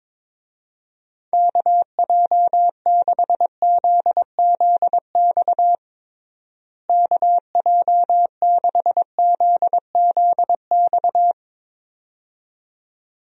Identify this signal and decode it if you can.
Morse code, heard as a keyed tone, about 22 wpm, 700 Hz. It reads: KJ6ZZX KJ6ZZX
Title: USA Call Signs - 6 Characters - Repeated 2x - 22wpm